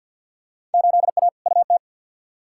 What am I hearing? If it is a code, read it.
Morse code, heard as a keyed tone, about 49 wpm, 700 Hz. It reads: 8A UT
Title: Field Day Exchanges Encore - 50wpm